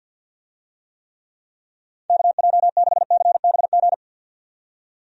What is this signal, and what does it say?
Morse code, heard as a keyed tone, about 50 wpm, 700 Hz. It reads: KJ6CBG
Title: USA Call Signs - 6 Characters - 50wpm